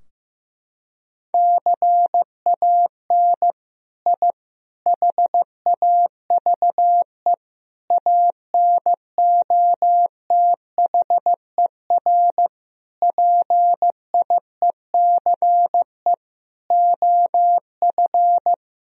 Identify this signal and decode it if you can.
Morse code, heard as a keyed tone, about 15 wpm, 700 Hz. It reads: CAN I HAVE ANOTHER PIECE OF
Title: Sentences from Top 1000 Words - Review Part 2 - 15wpm